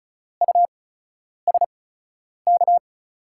Morse code, heard as a keyed tone, about 36 wpm, spaced as for 10 wpm, 700 Hz. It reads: USK
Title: USA Call Signs - 6 Characters - 8x Character Spacing - 35wpm